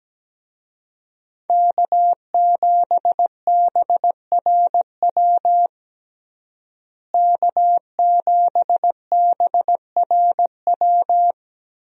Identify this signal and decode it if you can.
Morse code, heard as a keyed tone, about 17 wpm, 700 Hz. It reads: K7BRW K7BRW
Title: USA Call Signs - 5 Characters - Repeated 3x - 17wpm